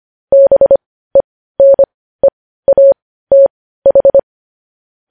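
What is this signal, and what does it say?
Morse code, heard as a keyed tone, about 25 wpm, spaced as for 15 wpm, 550 Hz. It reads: BENEATH
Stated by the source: Morse news headlines practice at 15.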